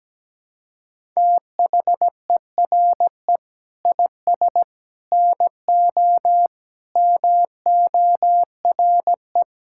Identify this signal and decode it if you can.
Morse code, heard as a keyed tone, about 17 wpm, 700 Hz. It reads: THERE IS NO MORE
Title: Sentences from Top 1000 Words - Review Part 6 - 17wpm